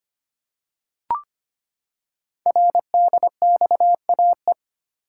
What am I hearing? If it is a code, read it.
Morse code, heard as a keyed tone, about 25 wpm, 700 Hz. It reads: RDXAE
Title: Code Groups - 5 Letters - 25wpm